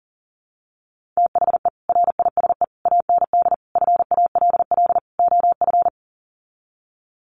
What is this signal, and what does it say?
Morse code, heard as a keyed tone, about 40 wpm, 700 Hz. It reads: THE RISE AND FALL OF
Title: Sets of 5 Words Encore2 40wpm